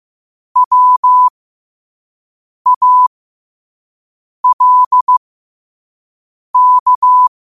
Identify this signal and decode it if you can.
Morse code, heard as a keyed tone, about 15 wpm, spaced as for 5 wpm, 1000 Hz. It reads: WALK
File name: qotdfiles / 5wpm